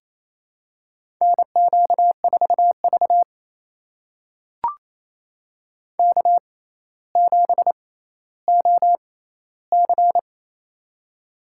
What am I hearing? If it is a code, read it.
Morse code, heard as a keyed tone, about 28 wpm, spaced as for 14 wpm, 700 Hz. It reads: NQ4V K7OC
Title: USA Call Signs - 4 Characters - 6x Character Spacing - 28wpm